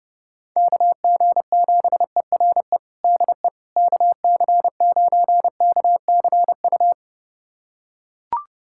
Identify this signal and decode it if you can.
Morse code, heard as a keyed tone, about 30 wpm, 700 Hz. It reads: KG7ERE DE KC9KCU
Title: USA Callsign DE Callsign 30wpm